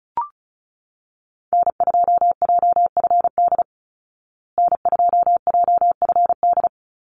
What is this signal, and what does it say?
Morse code, heard as a keyed tone, about 34 wpm, 700 Hz. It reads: N2JFD N2JFD
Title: Call Signs Repeated 2x - 35wpm